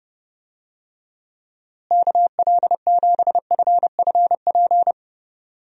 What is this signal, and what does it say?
Morse code, heard as a keyed tone, about 30 wpm, 700 Hz. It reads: KL7FFP
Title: USA Call Signs - 6 Characters - Repeated 3x - 30wpm